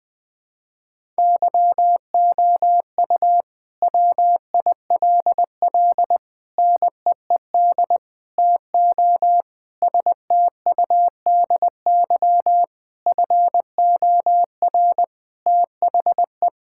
Morse code, heard as a keyed tone, about 20 wpm, 700 Hz. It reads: YOU WILL NEED TO STUDY FOR THE
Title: Sentences from Top 1000 Words - Review Part 5 - 20wpm